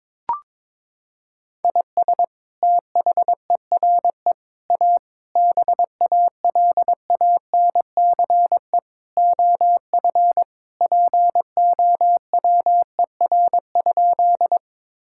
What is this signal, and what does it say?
Morse code, heard as a keyed tone, about 21 wpm, 700 Hz. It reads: IS THERE A BALANCE OF POWER?
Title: Sentences from Top 1000 Words - Review Part 6 - 22wpm